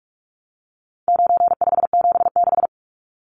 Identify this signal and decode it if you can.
Morse code, heard as a keyed tone, about 44 wpm, 700 Hz. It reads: 9576
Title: Numbers - 4-Digit 45wpm